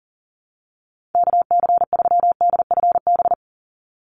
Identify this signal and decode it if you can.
Morse code, heard as a keyed tone, about 40 wpm, 700 Hz. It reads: KC3DFB
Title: USA Call Signs 40wpm